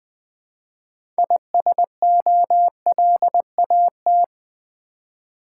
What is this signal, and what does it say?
Morse code, heard as a keyed tone, about 20 wpm, 700 Hz. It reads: ISOLAT
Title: Build A Word - 3 Letters At A Time - 9-Letter Words - 20wpm